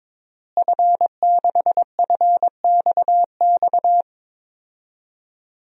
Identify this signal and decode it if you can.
Morse code, heard as a keyed tone, about 23 wpm, 700 Hz. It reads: F6FXX
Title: DXCC-100 Call Signs - 5 Characters - Repeated 3x - 22wpm